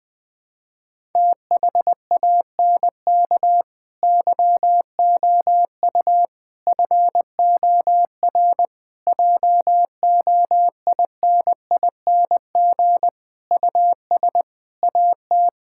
Morse code, heard as a keyed tone, about 21 wpm, 700 Hz. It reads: THANK YOU FOR JOINING US AT
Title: Sentences with 5-Word Phrases - Spoken Beginning - 20wpm